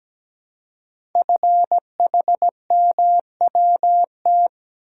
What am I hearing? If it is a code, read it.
Morse code, heard as a keyed tone, about 17 wpm, 700 Hz. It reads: FHMWT
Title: Code Groups - 5 Letters - 17wpm